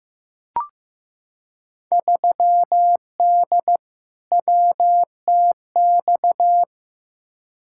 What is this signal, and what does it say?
Morse code, heard as a keyed tone, about 15 wpm, 700 Hz. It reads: 3D WTX
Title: Field Day Exchanges Encore - 15wpm